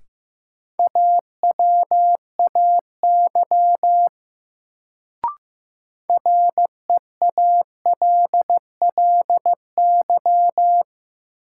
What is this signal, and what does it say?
Morse code, heard as a keyed tone, about 15 wpm, 700 Hz. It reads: AWAY REALLY
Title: Sentences from Top 600 Words - New Word Drill - 15wpm